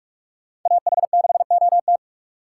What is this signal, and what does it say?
Morse code, heard as a keyed tone, about 40 wpm, 700 Hz. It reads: AH6OT